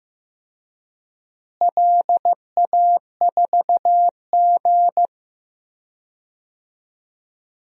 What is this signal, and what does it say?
Morse code, heard as a keyed tone, about 15 wpm, 700 Hz. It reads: LA4G
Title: DXCC-100 Call Signs - 4 Characters - Repeated 2x - 15wpm